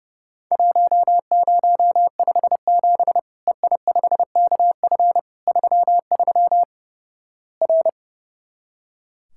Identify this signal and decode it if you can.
Morse code, heard as a keyed tone, about 30 wpm, 700 Hz. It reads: 1057 EI5KF 33 R
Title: Worked All Europe DX Contest - QTC Exchange - 30wpm